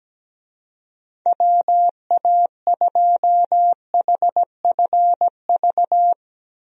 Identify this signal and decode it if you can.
Morse code, heard as a keyed tone, about 17 wpm, 700 Hz. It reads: WA2HFV
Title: USA Call Signs - 6 Characters - 17wpm